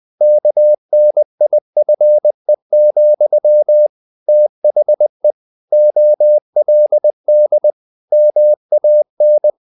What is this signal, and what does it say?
Morse code, heard as a keyed tone, about 20 wpm, 600 Hz. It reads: KNIFE, THE OLD MAN
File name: the old man and the sea part 2 20wpm